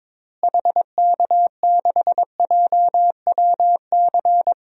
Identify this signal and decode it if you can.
Morse code, heard as a keyed tone, about 21 wpm, 700 Hz. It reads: HK6JWC
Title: DXCC-100 Call Signs - 6 Characters - Repeated 3x - 22wpm